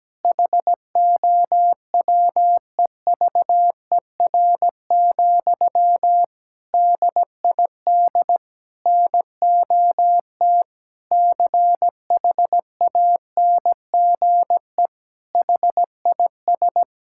Morse code, heard as a keyed tone, about 17 wpm, 700 Hz. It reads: HOWEVER, DID NOT CHANGE HIS
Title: Sentences from Top 1000 Words - Review Part 5 - 17wpm